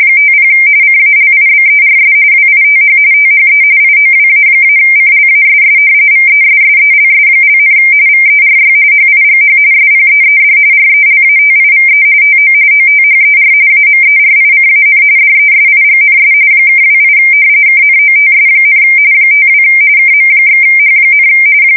RTTY, about 45.45 baud. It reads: 50 CALIBERS WITH THE IOWA-CLASS VESSELS), VERY HIGH SPEED RELATIVE TO OTHER AMERICAN DESIGNS (28 KNOTS, 52 KM/H,
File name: fsk170 / battleships